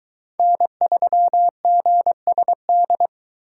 Morse code, heard as a keyed tone, about 22 wpm, 700 Hz. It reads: N3GSD